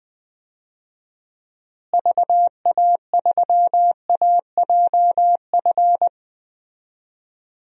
Morse code, heard as a keyed tone, about 21 wpm, 700 Hz. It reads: VA3AJF
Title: DXCC-100 Call Signs - 6 Characters - Repeated 3x - 20wpm